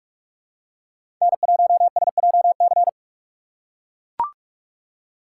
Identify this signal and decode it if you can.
Morse code, heard as a keyed tone, about 44 wpm, 690 Hz. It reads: N1SJC
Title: USA Call Signs - 5 Characters - 6x Character Spacing - 45wpm